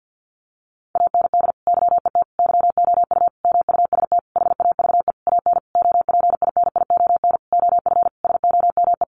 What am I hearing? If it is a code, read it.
Morse code, heard as a keyed tone, about 50 wpm, 700 Hz. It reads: AND YET YOU MUST HAVE AN OPINION OF SOME
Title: Sentences from Top 1300 Words 50wpm